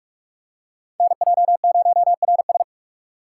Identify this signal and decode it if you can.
Morse code, heard as a keyed tone, about 43 wpm, 700 Hz. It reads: NJ0RS